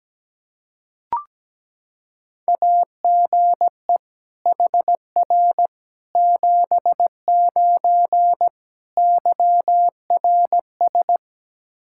Morse code, heard as a keyed tone, about 17 wpm, 700 Hz. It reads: AGE HR 79 YRS
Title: Practice QSO - Exchange 3 - More Info - 17wpm